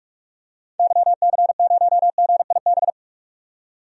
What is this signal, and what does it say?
Morse code, heard as a keyed tone, about 46 wpm, 700 Hz. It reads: YC0GIB
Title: DXCC-100 Call Signs - 6 Characters - 45wpm